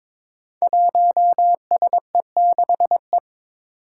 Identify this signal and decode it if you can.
Morse code, heard as a keyed tone, about 23 wpm, 700 Hz. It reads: 1SE6E
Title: Five Challenging Letter-Number Combinations 22wpm